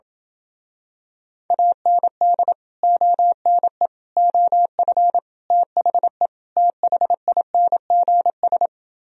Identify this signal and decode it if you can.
Morse code, heard as a keyed tone, about 28 wpm, 700 Hz. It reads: AND ONE OF THE THINGS